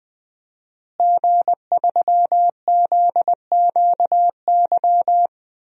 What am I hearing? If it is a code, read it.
Morse code, heard as a keyed tone, about 19 wpm, 700 Hz. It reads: G3ZQY
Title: DXCC-100 Call Signs - 5 Characters - Repeated 3x - 20wpm